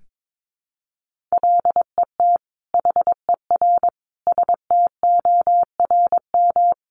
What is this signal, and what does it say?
Morse code, heard as a keyed tone, about 22 wpm, 700 Hz. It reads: LET HER STORM
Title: Sentences from Top 1500 Words 22wpm